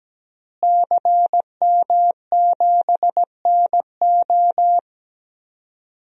Morse code, heard as a keyed tone, about 17 wpm, 700 Hz. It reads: CM7NO
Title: DXCC-100 Call Signs - 5 Characters - Repeated 3x - 17wpm